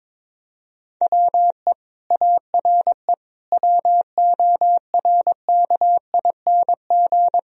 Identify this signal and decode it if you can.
Morse code, heard as a keyed tone, about 22 wpm, 700 Hz. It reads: WE ARE WORKING
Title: Sentences from Top 1000 Words - Review Part 4 - 22wpm